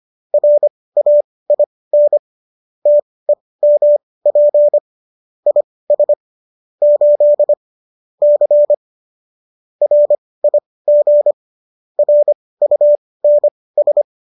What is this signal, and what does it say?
Morse code, heard as a keyed tone, about 25 wpm, spaced as for 18 wpm, 590 Hz. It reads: RAIN TEMP IS 8 C RIG RUNS
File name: qso205 18